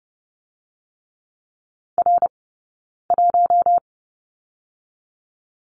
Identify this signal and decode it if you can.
Morse code, heard as a keyed tone, about 30 wpm, 700 Hz. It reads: R 1
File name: 092 Lesson 16d - Letter c - Two Characters 30wpm